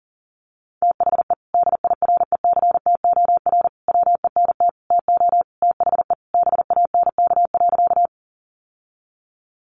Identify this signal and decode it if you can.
Morse code, heard as a keyed tone, about 40 wpm, 700 Hz. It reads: THE DIRECTOR WENT TO THE BANK.
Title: Sentences from Top 1300 Words 40wpm